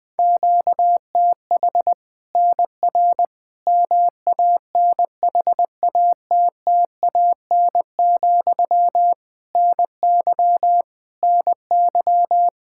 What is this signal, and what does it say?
Morse code, heard as a keyed tone, about 20 wpm, 700 Hz. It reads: QTH NR MANHATTAN, NY NY
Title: Practice QSO - Exchange 1 - Beginning - 20wpm